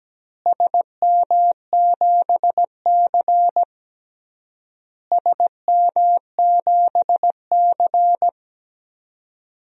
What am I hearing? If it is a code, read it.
Morse code, heard as a keyed tone, about 17 wpm, 700 Hz. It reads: SM7C SM7C
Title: DXCC-100 Call Signs - 4 Characters - Repeated 3x - 17wpm